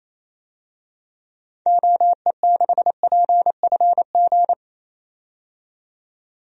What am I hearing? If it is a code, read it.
Morse code, heard as a keyed tone, about 27 wpm, 700 Hz. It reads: OE6PFG
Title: DXCC-100 Call Signs - 6 Characters - 28wpm